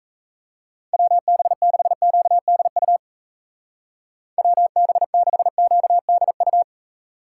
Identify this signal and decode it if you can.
Morse code, heard as a keyed tone, about 35 wpm, 700 Hz. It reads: WB6QDU WB6QDU